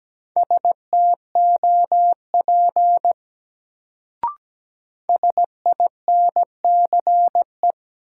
Morse code, heard as a keyed tone, about 17 wpm, 700 Hz. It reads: STOP SINCE
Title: Sentences from Top 300 Words - New Word Drill - 17wpm